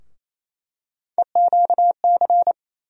Morse code, heard as a keyed tone, about 28 wpm, 700 Hz. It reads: EQC
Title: USA Call Sign Postfixes - 3 Letters 28wpm